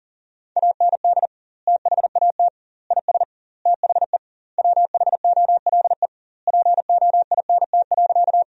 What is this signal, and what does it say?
Morse code, heard as a keyed tone, about 40 wpm, 700 Hz. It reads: AND THAT IS THE WHOLE POINT.
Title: 2x Word-Spacing - Sentences from Top 400 Words - 40wpm